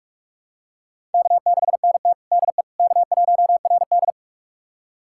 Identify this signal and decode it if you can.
Morse code, heard as a keyed tone, about 45 wpm, 700 Hz. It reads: K6NT DE K1RD
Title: USA Callsign DE Callsign - 4 Characters - 45wpm